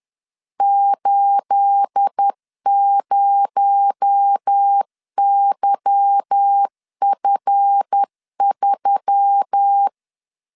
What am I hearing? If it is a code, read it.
Morse code, heard as a keyed tone, about 10 wpm, 800 Hz. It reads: 80YF3
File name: LEZIONI / 9 28 10wpm 2842